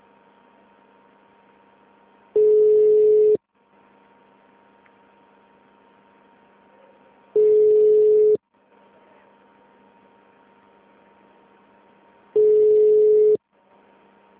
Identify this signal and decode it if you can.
Ringback tone (Europe, 425 Hz).